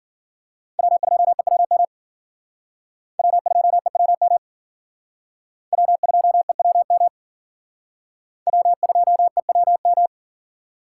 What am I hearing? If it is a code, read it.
Morse code, heard as a keyed tone, about 54 wpm, 700 Hz. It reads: W2EWM W2EWM W2EWM W2EWM